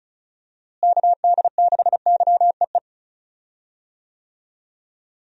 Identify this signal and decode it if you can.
Morse code, heard as a keyed tone, about 36 wpm, 700 Hz. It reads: KD6YEE
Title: USA Call Signs - 6 Characters - Repeated 3x - 35wpm